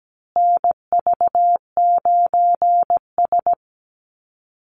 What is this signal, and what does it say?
Morse code, heard as a keyed tone, about 17 wpm, 700 Hz. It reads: NV9S
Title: USA Call Signs Repeated 3x 17wpm